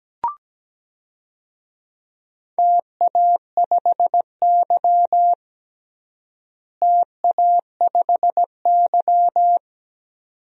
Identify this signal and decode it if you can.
Morse code, heard as a keyed tone, about 17 wpm, 700 Hz. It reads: TA5Y TA5Y
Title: DXCC-100 Call Signs - 4 Characters - Repeated 2x - 17wpm